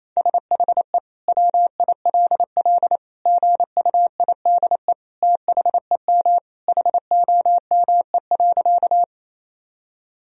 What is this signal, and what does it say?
Morse code, heard as a keyed tone, about 28 wpm, 700 Hz. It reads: SHE WILL GUIDE THEM HOME.
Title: Sentences from Top 1000 Words - Review Part 3 - 28wpm